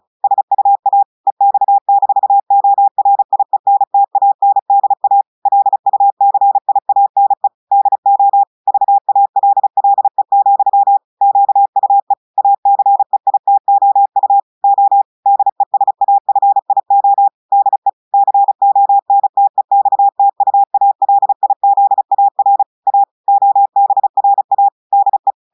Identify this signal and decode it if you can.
Morse code, heard as a keyed tone, about 36 wpm, 800 Hz. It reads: SUA EX-ORIENTANDA LUCIANE DO VALLE, QUE ACEITOU O DESAFIO DE CONTEXTUALIZAR A OBRA DE
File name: audios / portugais / portugais texte en clair 1 vitesse 35 mots minutes